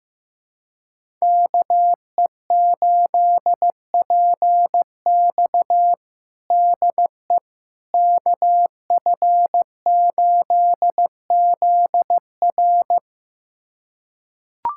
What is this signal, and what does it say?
Morse code, heard as a keyed tone, about 15 wpm, 700 Hz. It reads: KE8PX DE KF8ZR
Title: USA Callsign DE Callsign - 5 Characters - 15wpm